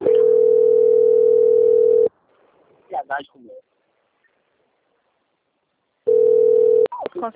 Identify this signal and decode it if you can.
Ringback tone (North America), 440 + 480 Hz, with speech.
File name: out- - - - - .42535